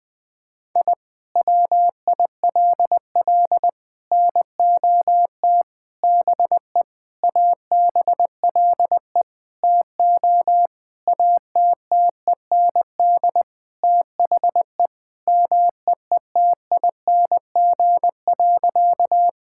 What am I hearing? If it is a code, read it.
Morse code, heard as a keyed tone, about 20 wpm, 700 Hz. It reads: I WILL NOT BE ABLE TO ATTEND THE MEETING.
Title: Sentences with 5-Word Phrases - 20wpm